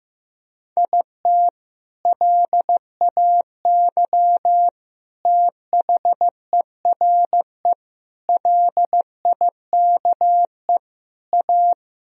Morse code, heard as a keyed tone, about 15 wpm, 700 Hz. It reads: IT LAY THERE LIKE A
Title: Sentences from Top 1000 Words - Review Part 3 - 15wpm